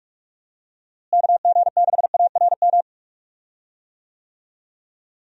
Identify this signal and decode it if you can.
Morse code, heard as a keyed tone, about 46 wpm, 700 Hz. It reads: KG6ARM
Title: USA Call Signs - 6 Characters - Repeated 2x - 45wpm